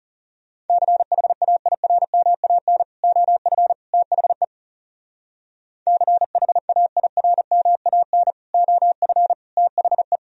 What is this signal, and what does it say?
Morse code, heard as a keyed tone, about 35 wpm, 700 Hz. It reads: CHAIRMAN OF THE CHAIRMAN OF THE
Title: Speed-Racing - Sets of 3 Words 30wpm